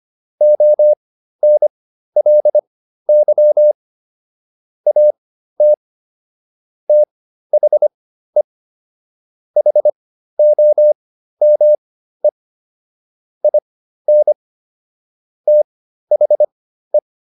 Morse code, heard as a keyed tone, about 25 wpm, spaced as for 13 wpm, 600 Hz. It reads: ONLY AT THE HOME IN THE